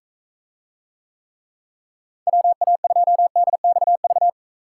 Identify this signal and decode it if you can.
Morse code, heard as a keyed tone, about 45 wpm, 700 Hz. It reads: WA2DXV